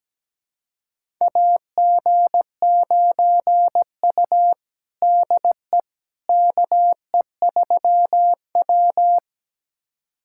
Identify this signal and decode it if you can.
Morse code, heard as a keyed tone, about 17 wpm, 700 Hz. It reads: AG9U DE KE3W
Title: USA Callsign DE Callsign - 4 Characters - 17wpm